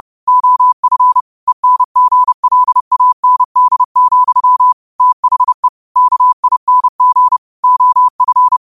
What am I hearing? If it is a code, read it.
Morse code, heard as a keyed tone, about 30 wpm, 1000 Hz. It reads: OF ENGLAND, THE KING OF
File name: qotdfiles / 30wpm